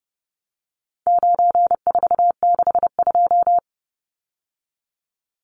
Morse code, heard as a keyed tone, about 30 wpm, 700 Hz. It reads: 9462
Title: Numbers - 4-Digit 30wpm